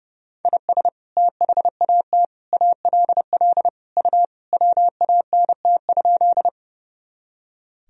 Morse code, heard as a keyed tone, about 30 wpm, 700 Hz. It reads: IS THAT ALL U WANT?
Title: Sentences from Top 200 Words and Abbreviations 30wpm